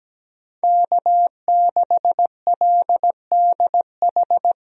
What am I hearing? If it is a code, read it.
Morse code, heard as a keyed tone, about 17 wpm, 700 Hz. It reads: K6LDH
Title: USA Call Signs - 5 Characters - Repeated 2x - 17wpm